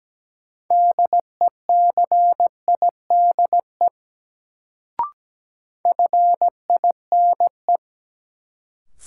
Morse code, heard as a keyed tone, about 17 wpm, 700 Hz. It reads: DECIDE FINE
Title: Sentences from Top 500 Words - New Word Drill - 17wpm